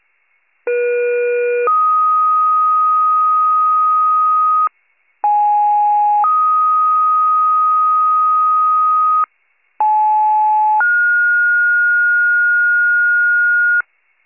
Two-tone page, heard as a one-second tone, then a three-second tone, about 480 then 1250 Hz, about 830 then 1250 Hz, about 860 then 1430 Hz. It is heard from 0.7 to 13.8 s.